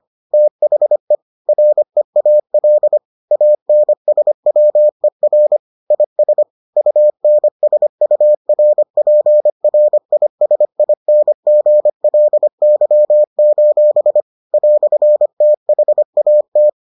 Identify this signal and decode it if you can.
Morse code, heard as a keyed tone, about 25 wpm, 600 Hz. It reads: THE REAL ANSWER IS UNSURPRISINGLY: "THAT